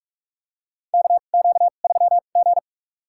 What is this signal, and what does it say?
Morse code, heard as a keyed tone, about 46 wpm, spaced as for 33 wpm, 700 Hz. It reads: KQ3G
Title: USA Call Signs - 2x Character Spacing - 45wpm